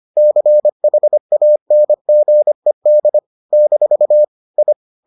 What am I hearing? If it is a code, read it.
Morse code, heard as a keyed tone, about 25 wpm, 600 Hz. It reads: CHANGED - I